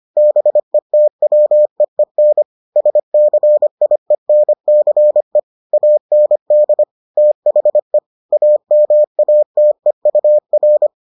Morse code, heard as a keyed tone, about 25 wpm, 600 Hz. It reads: BETWEEN SCIENCE AND THE AMATEUR